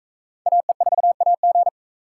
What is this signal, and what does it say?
Morse code, heard as a keyed tone, about 40 wpm, 700 Hz. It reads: AE4AG